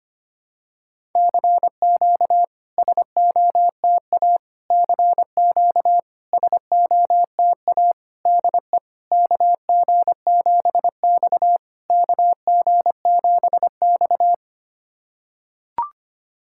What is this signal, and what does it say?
Morse code, heard as a keyed tone, about 25 wpm, 690 Hz. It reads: CQ SOTA CQ SOTA DE KG7X KG7X
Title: SOTA QSO Practice 25wpm